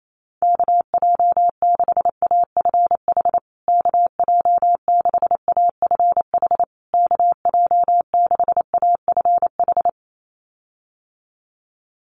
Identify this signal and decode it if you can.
Morse code, heard as a keyed tone, about 28 wpm, 700 Hz. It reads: KJ6AFH KJ6AFH KJ6AFH
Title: USA Call Signs Repeated 3x 28wpm